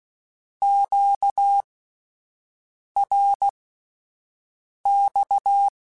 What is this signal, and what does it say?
Morse code, heard as a keyed tone, about 16 wpm, spaced as for 5 wpm, 800 Hz. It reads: QRX
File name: Misc / Misc-mp3s / Words @5